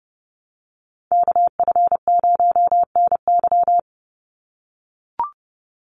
Morse code, heard as a keyed tone, about 30 wpm, 700 Hz. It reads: KF0NY
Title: USA Call Signs 30wpm